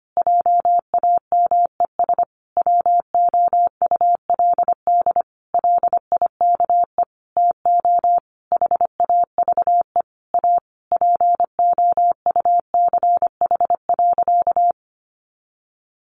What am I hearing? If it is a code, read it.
Morse code, heard as a keyed tone, about 25 wpm, 700 Hz. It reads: JAMES WOULD LIKE TO HAVE A POUCH.
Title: Easy Phrases 25wpm